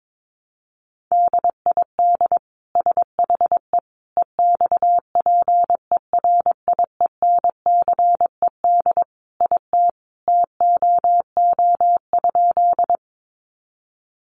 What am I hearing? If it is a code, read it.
Morse code, heard as a keyed tone, about 22 wpm, 700 Hz. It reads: DID SHE EXPERIENCED IT TOO?
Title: Sentences from Top 1100 Words 22wpm